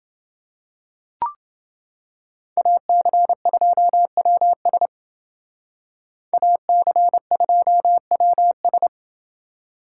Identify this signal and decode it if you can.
Morse code, heard as a keyed tone, about 29 wpm, 700 Hz. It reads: AC2WS AC2WS